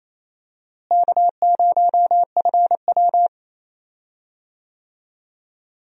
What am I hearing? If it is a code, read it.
Morse code, heard as a keyed tone, about 28 wpm, 700 Hz. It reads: K0FW
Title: USA Call Signs - 4 Characters - Repeated 3x - 28wpm